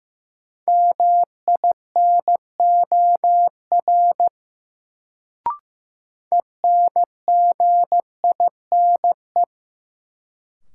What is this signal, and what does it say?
Morse code, heard as a keyed tone, about 15 wpm, 700 Hz. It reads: MINOR ENGINE
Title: Sentences from Top 1500 Words - New Word Drill Alt - 15wpm